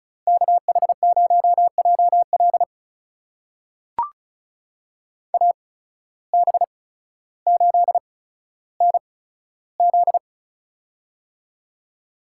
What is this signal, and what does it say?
Morse code, heard as a keyed tone, about 36 wpm, spaced as for 15 wpm, 700 Hz. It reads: KH0JL AB8NZ
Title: USA Call Signs - 5 Characters - 8x Character Spacing - 35wpm